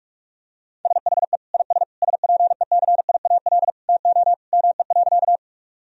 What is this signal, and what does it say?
Morse code, heard as a keyed tone, about 46 wpm, 700 Hz. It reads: SHE IS SPECIAL TO ME.
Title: Sentences from Top 1000 Words - Review Part 3 - 45wpm